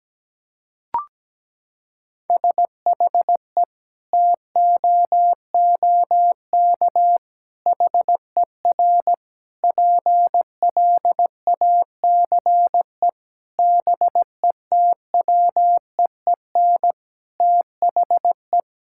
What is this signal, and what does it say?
Morse code, heard as a keyed tone, about 17 wpm, 700 Hz. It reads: SHE TOOK HER PLACE BETWEEN THE
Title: Sentences from Top 1000 Words - Review Part 5 - 17wpm